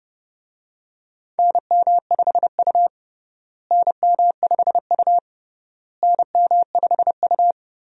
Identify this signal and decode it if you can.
Morse code, heard as a keyed tone, about 30 wpm, 700 Hz. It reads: NM5U NM5U NM5U
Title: USA Call Signs - 4 Characters - Repeated 3x - 30wpm